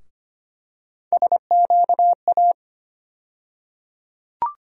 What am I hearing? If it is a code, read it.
Morse code, heard as a keyed tone, about 25 wpm, 700 Hz. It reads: SQA
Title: USA Call Sign Postfixes - 3 Letters 25wpm